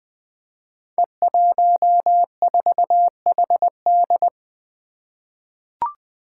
Morse code, heard as a keyed tone, about 21 wpm, 700 Hz. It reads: E14HD
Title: Five Challenging Letter-Number Combinations 20wpm